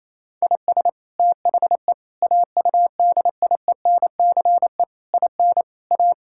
Morse code, heard as a keyed tone, about 28 wpm, 700 Hz. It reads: IS THE AUDIENCE IN A
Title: Sentences from Top 1000 Words - Review Part 5 - 28wpm